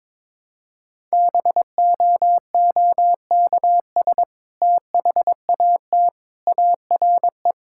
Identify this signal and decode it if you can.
Morse code, heard as a keyed tone, about 23 wpm, 700 Hz. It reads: BOOKS THAT ARE
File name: practice / 22wpm / Sentences with 2-Word Phrases - 22wpm